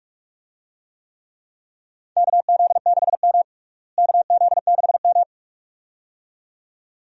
Morse code, heard as a keyed tone, about 46 wpm, 700 Hz. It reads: KZ6M KZ6M
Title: USA Call Signs - 4 Characters - Repeated 2x - 45wpm